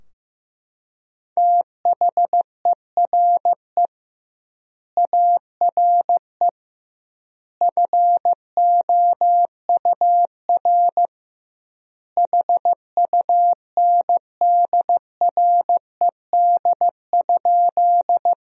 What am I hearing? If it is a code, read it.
Morse code, heard as a keyed tone, about 15 wpm, 700 Hz. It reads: THERE ARE FOUR HUNDRED?
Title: 2x Word-Spacing - Sentences from Top 400 Words - 15wpm